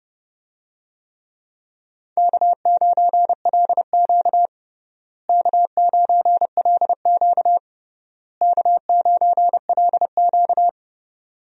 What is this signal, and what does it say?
Morse code, heard as a keyed tone, about 32 wpm, 700 Hz. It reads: K9LQ K9LQ K9LQ
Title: USA Call Signs - 4 Characters - Repeated 3x - 30wpm